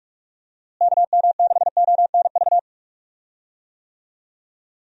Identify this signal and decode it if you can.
Morse code, heard as a keyed tone, about 44 wpm, 700 Hz. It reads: KM6ONV
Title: USA Call Signs - 6 Characters - Repeated 3x - 45wpm